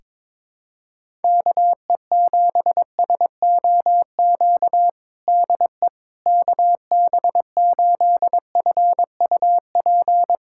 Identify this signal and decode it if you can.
Morse code, heard as a keyed tone, about 22 wpm, 700 Hz. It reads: KE7SOQ DE KB8FUP
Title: USA Callsign DE Callsign 22wpm